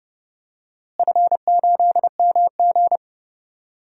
Morse code, heard as a keyed tone, about 32 wpm, 700 Hz. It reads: F8MG
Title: DXCC-100 Call Signs - 4 Characters - Repeated 2x - 30wpm